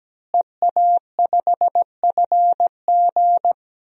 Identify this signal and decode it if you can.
Morse code, heard as a keyed tone, about 17 wpm, 700 Hz. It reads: EA5FG
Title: DXCC-100 Call Signs - 5 Characters - Repeated 2x - 17wpm